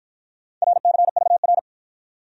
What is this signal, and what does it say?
Morse code, heard as a keyed tone, about 50 wpm, 700 Hz. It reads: RCVR